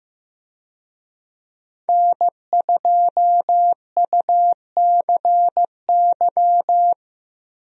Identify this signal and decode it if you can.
Morse code, heard as a keyed tone, about 15 wpm, 700 Hz. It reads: N2UCY
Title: USA Call Signs - 5 Characters - 15wpm